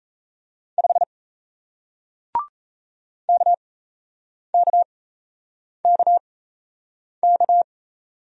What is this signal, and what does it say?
Morse code, heard as a keyed tone, about 38 wpm, spaced as for 8 wpm, 700 Hz. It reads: 5 KKKK